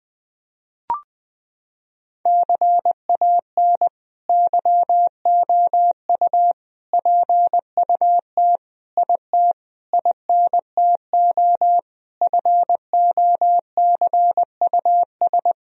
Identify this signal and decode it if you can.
Morse code, heard as a keyed tone, about 20 wpm, 700 Hz. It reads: CAN YOU PUT IT INTO FOCUS
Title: Sentences from Top 1000 Words - Review Part 4 - 20wpm